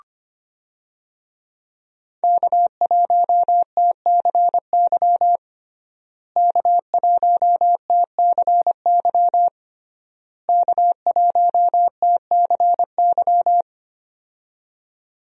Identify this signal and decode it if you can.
Morse code, heard as a keyed tone, about 25 wpm, 700 Hz. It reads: K1TCY K1TCY K1TCY
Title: USA Call Signs - 5 Characters - Repeated 3x - 25wpm